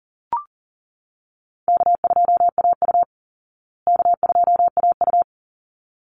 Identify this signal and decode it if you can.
Morse code, heard as a keyed tone, about 40 wpm, 700 Hz. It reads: K2AU K2AU
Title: Call Signs Repeated 3x - 40wpm